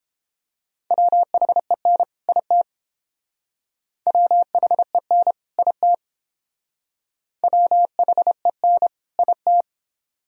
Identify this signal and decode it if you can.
Morse code, heard as a keyed tone, about 29 wpm, 700 Hz. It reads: WHEN IT WHEN IT WHEN IT